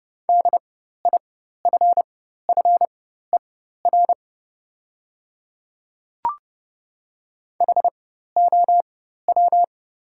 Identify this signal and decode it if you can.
Morse code, heard as a keyed tone, about 30 wpm, spaced as for 14 wpm, 690 Hz. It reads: DIFFER HOW
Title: 4x Character Spacing - Top 200 Words - 30wpm